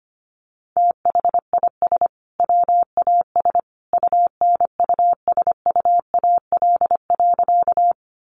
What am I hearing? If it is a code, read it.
Morse code, heard as a keyed tone, about 25 wpm, 700 Hz. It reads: THIS WAS UNUSUAL.
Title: Sentences from Top 1400 Words 25wpm